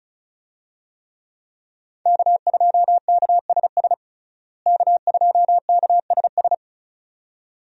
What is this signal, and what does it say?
Morse code, heard as a keyed tone, about 34 wpm, 700 Hz. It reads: K2KSS K2KSS
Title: USA Call Signs - 5 Characters - Repeated 2x - 35wpm